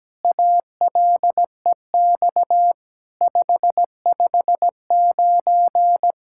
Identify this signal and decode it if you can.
Morse code, heard as a keyed tone, about 17 wpm, 700 Hz. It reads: ALEX 559
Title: SOTA QSO Practice 17wpm